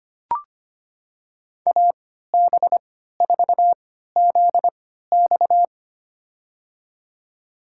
Morse code, heard as a keyed tone, about 25 wpm, spaced as for 14 wpm, 700 Hz. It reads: AB4ZX
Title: USA Call Signs - 5 Characters - 3x Character Spacing - 25wpm